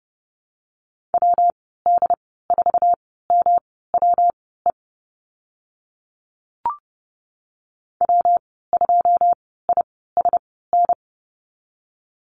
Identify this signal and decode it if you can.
Morse code, heard as a keyed tone, about 30 wpm, spaced as for 17 wpm, 700 Hz. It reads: WD4MWE W2ISN
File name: Lesson 19g - Letter w - Call Signs 30wpm